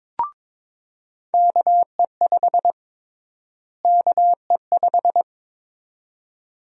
Morse code, heard as a keyed tone, about 22 wpm, 700 Hz. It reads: KE5 KE5
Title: USA Call Sign Prefixes - Repeated 2x - 22wpm